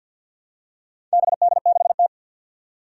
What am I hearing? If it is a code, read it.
Morse code, heard as a keyed tone, about 53 wpm, 700 Hz. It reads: BD6T
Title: Four Challenging Letter-Number Combinations 50wpm